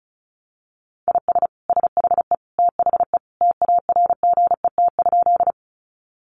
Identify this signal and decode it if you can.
Morse code, heard as a keyed tone, about 34 wpm, 700 Hz. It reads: IS SHE THE TARGET?
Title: Sentences from Top 1200 Words 35wpm